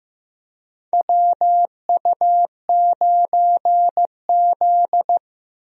Morse code, heard as a keyed tone, about 15 wpm, 700 Hz. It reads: WU9Z
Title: USA Call Signs - 4 Characters - Repeated 3x - 15wpm